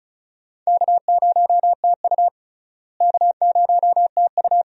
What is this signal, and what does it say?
Morse code, heard as a keyed tone, about 34 wpm, 700 Hz. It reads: K0TU K0TU
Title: USA Call Signs - 4 Characters - Repeated 2x - 35wpm